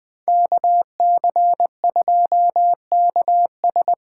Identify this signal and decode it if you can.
Morse code, heard as a keyed tone, about 20 wpm, 700 Hz. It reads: KC2KS
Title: USA Call Signs - 8x Character Spacing - 20wpm